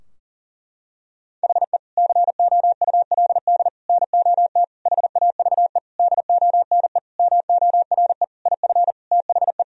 Morse code, heard as a keyed tone, about 37 wpm, 700 Hz. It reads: HE COULD NOT HAVE DONE MORE IF THE
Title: Sentences from Top 1000 Words - Review Part 2 - 40wpm